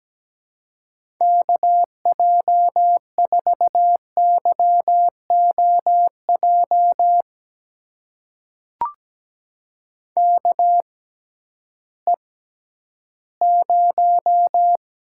Morse code, heard as a keyed tone, about 17 wpm, spaced as for 11 wpm, 700 Hz. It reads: KJ4YOJ KE0
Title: USA Call Signs - 6 Characters - 6x Character Spacing - 17wpm